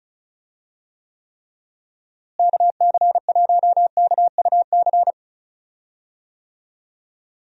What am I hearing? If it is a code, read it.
Morse code, heard as a keyed tone, about 34 wpm, 700 Hz. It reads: KC1KUC
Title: USA Call Signs - 6 Characters - Repeated 3x - 35wpm